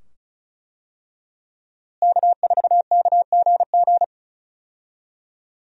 Morse code, heard as a keyed tone, about 36 wpm, 700 Hz. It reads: K4KGG
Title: USA Call Signs - 5 Characters - Repeated 2x - 35wpm